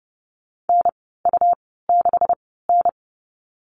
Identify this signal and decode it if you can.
Morse code, heard as a keyed tone, about 30 wpm, spaced as for 17 wpm, 700 Hz. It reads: NU6N
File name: Lesson 37g - Letter x - Call Signs 30wpm